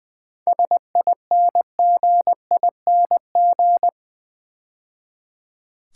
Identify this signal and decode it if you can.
Morse code, heard as a keyed tone, about 20 wpm, 700 Hz. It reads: SINGING
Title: 7 Letter Words - First 3 Letters Given - 20wpm